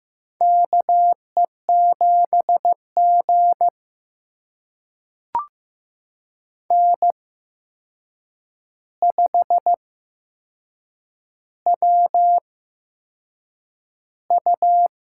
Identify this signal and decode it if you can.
Morse code, heard as a keyed tone, about 15 wpm, spaced as for 6 wpm, 700 Hz. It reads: KE7G N5WU
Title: USA Call Signs - 8x Character Spacing - 15wpm